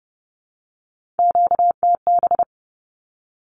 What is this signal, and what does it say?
Morse code, heard as a keyed tone, about 30 wpm, 700 Hz. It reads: QTB
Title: Q-Codes 30wpm